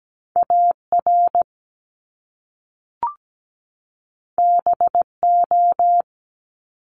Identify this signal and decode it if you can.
Morse code, heard as a keyed tone, about 17 wpm, 700 Hz. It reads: AR BO
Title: Top Two Letter Combinations in Words 17wpm